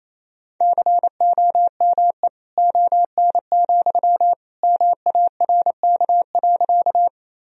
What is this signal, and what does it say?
Morse code, heard as a keyed tone, about 28 wpm, 700 Hz. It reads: COME ON, MARK.
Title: Sentences from Top 1000 Words - Review Part 1 - 28wpm